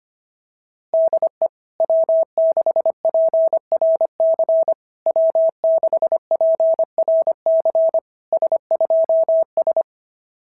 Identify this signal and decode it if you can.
Morse code, heard as a keyed tone, about 24 wpm, 650 Hz. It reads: DE W6PRC W6PRC S2S
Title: SOTA QSO Practice 25wpm